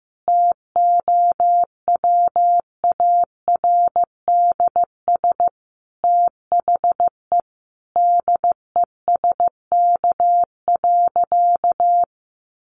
Morse code, heard as a keyed tone, about 15 wpm, 700 Hz. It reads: TOWARDS THE DESK.
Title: Additional Easy Phrases 15wpm